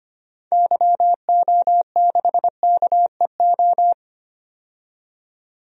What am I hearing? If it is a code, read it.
Morse code, heard as a keyed tone, about 25 wpm, 700 Hz. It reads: YO6KEO
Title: DXCC-100 Call Signs - 6 Characters - 25wpm